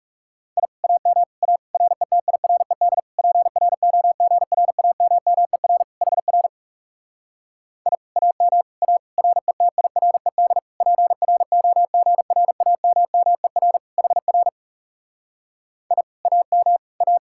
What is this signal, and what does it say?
Morse code, heard as a keyed tone, about 40 wpm, 700 Hz. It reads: I AM A RETIRED PROGRAMMER HR I AM A RETIRED PROGRAMMER HR I AM A